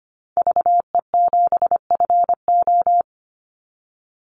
Morse code, heard as a keyed tone, about 25 wpm, 700 Hz. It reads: VE7FO
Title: Call Signs Repeated 3x - 25wpm